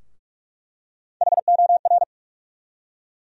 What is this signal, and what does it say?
Morse code, heard as a keyed tone, about 46 wpm, 700 Hz. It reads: HOR
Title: USA Call Sign Postfixes - 3 Letters 45wpm